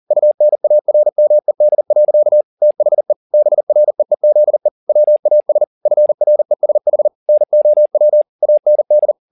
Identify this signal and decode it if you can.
Morse code, heard as a keyed tone, about 40 wpm, 600 Hz. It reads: UNARMED. THE BREEZE WAS FRESH NOW AND
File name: the old man and the sea 2 40wpm